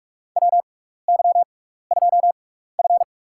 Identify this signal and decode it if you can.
Morse code, heard as a keyed tone, about 44 wpm, spaced as for 15 wpm, 700 Hz. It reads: WY2F
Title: USA Call Signs - 6x Character Spacing - 45wpm